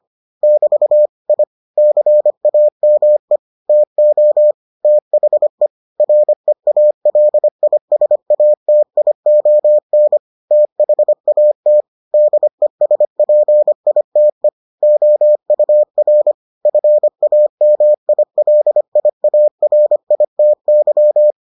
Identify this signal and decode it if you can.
Morse code, heard as a keyed tone, about 25 wpm, 600 Hz. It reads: <BT> I CAME TO THE REALISATION THAT DESPITE OUR FAMILIARITY